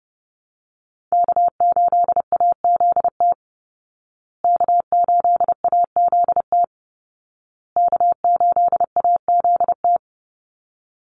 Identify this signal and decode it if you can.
Morse code, heard as a keyed tone, about 30 wpm, 700 Hz. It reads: K8AZT K8AZT K8AZT
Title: Call Signs Repeated 2x - 30wpm